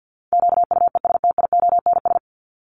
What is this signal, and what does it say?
Morse code, heard as a keyed tone, about 50 wpm, 700 Hz. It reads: QUESTIONS
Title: ES:S Suffix - Top 100 Words with Base Words 50wpm